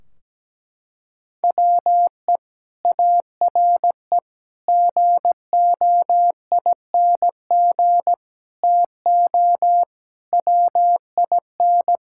Morse code, heard as a keyed tone, about 17 wpm, 700 Hz. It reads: WE ARE GOING TO WIN
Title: Sentences from Top 1600 Words 17wpm